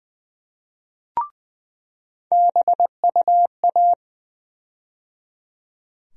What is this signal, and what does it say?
Morse code, heard as a keyed tone, about 20 wpm, 700 Hz. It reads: BUA
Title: USA Call Sign Postfixes - 3 Letters 20wpm